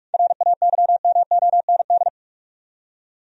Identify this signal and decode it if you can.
Morse code, heard as a keyed tone, about 42 wpm, 700 Hz. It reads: RAYMOND
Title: USA Common Names - 45wpm